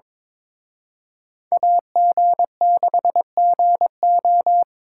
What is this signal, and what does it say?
Morse code, heard as a keyed tone, about 26 wpm, 700 Hz. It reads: AG6GO
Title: Speed Racing - USA Call Signs 22wpm